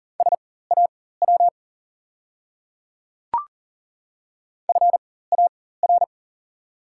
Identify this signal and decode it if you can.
Morse code, heard as a keyed tone, about 38 wpm, spaced as for 18 wpm, 700 Hz. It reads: SAW FAR
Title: 4x Character Spacing - Top 500 Words - 40wpm